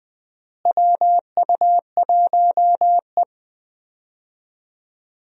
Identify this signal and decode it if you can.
Morse code, heard as a keyed tone, about 20 wpm, 700 Hz. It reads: WU1E
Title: USA Call Signs - 4 Characters - Repeated 3x - 20wpm